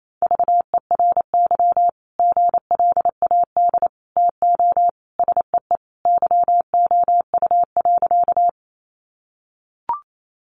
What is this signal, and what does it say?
Morse code, heard as a keyed tone, about 28 wpm, 700 Hz. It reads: VERY GLAD TO SEE YOU.
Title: Sentences from Top 800 Words 28wpm